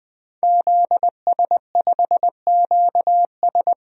Morse code, heard as a keyed tone, about 20 wpm, 700 Hz. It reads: ZS5QS
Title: DXCC-100 Call Signs - 5 Characters - 20wpm